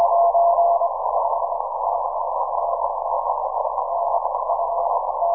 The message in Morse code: none